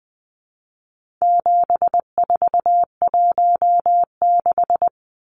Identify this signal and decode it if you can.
Morse code, heard as a keyed tone, about 20 wpm, 700 Hz. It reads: 7416
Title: Numbers - 4-Digit 20wpm